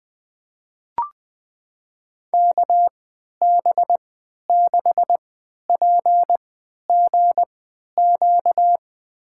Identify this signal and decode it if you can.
Morse code, heard as a keyed tone, about 20 wpm, spaced as for 11 wpm, 700 Hz. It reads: KB6PGQ
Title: USA Call Signs - 6 Characters - 3x Character Spacing - 20wpm